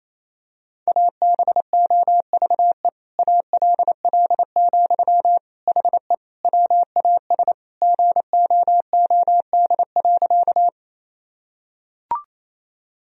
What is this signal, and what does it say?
Morse code, heard as a keyed tone, about 28 wpm, 700 Hz. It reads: ABOVE ALL, HE WAS GOOD.
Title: Sentences from Top 1000 Words - Review Part 2 - 28wpm